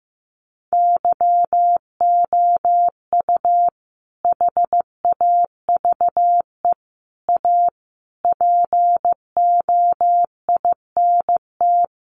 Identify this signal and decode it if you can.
Morse code, heard as a keyed tone, about 15 wpm, 700 Hz. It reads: YOU HAVE A POINT
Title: Sentences from Top 1400 Words 15wpm